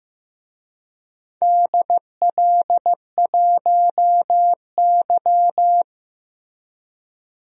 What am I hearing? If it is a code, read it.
Morse code, heard as a keyed tone, about 15 wpm, 700 Hz. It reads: DL1Y
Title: DXCC-100 Call Signs - 4 Characters - 15wpm